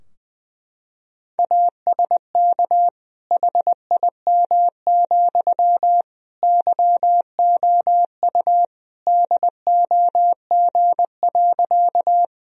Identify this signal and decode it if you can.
Morse code, heard as a keyed tone, about 20 wpm, 700 Hz. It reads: ASK HIM, YOU DOG.
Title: Sentences from Top 1000 Words - Review Part 2 - 20wpm